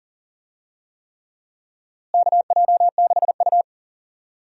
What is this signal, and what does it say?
Morse code, heard as a keyed tone, about 40 wpm, 700 Hz. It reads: KJ6U
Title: USA Call Signs - 4 Characters - Repeated 3x - 40wpm